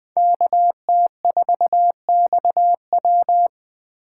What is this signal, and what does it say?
Morse code, heard as a keyed tone, about 20 wpm, 700 Hz. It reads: KT4XW
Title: Speed Racing - USA 2x2 Call Signs 17wpm